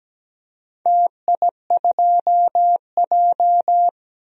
Morse code, heard as a keyed tone, about 17 wpm, 700 Hz. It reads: TI2J
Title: DXCC-100 Call Signs - 4 Characters - 17wpm